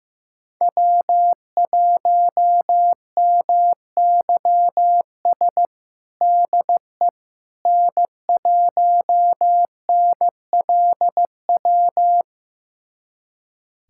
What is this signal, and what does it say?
Morse code, heard as a keyed tone, about 15 wpm, 700 Hz. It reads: W1MYS DE N1NLW
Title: USA Callsign DE Callsign - 5 Characters - 15wpm